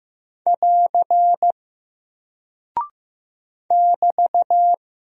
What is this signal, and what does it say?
Morse code, heard as a keyed tone, about 15 wpm, 700 Hz. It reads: <AR> <BT>
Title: Single Character (Letter:Number:Punctuation:Prosign) Alt 15wpm